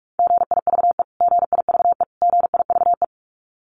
Morse code, heard as a keyed tone, about 44 wpm, 700 Hz. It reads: GIVE GIVE GIVE
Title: Top 500 Words Repeated 45wpm